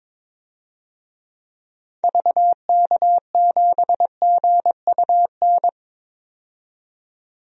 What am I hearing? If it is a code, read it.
Morse code, heard as a keyed tone, about 22 wpm, 700 Hz. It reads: VK7GUN
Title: DXCC-100 Call Signs - 6 Characters - 22wpm